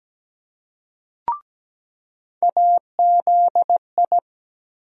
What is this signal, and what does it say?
Morse code, heard as a keyed tone, about 17 wpm, 700 Hz. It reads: AZI
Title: USA Call Sign Postfixes - 3 Letters 17wpm